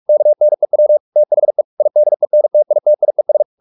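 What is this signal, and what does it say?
Morse code, heard as a keyed tone, about 44 wpm, 600 Hz. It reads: KNEW THE IDENTITIES
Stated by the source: Morse practice at 45 wpm, the novel Never by Ken Follett in English, from CW med Gnister.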